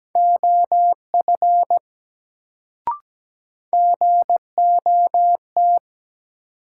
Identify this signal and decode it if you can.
Morse code, heard as a keyed tone, about 17 wpm, 700 Hz. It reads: OF GOT
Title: Sets of 2 Words - New Word Drill - 17wpm